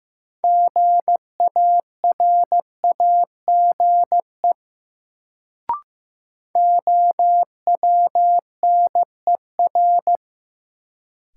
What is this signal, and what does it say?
Morse code, heard as a keyed tone, about 15 wpm, 700 Hz. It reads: GARAGE OWNER
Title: Sentences from Top 1600 Words - New Word Drill - 15wpm